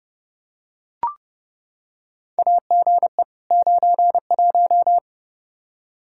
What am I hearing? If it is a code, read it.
Morse code, heard as a keyed tone, about 31 wpm, 700 Hz. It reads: AGE 91
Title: Practice QSO - Exchange 3 - More Info - 30wpm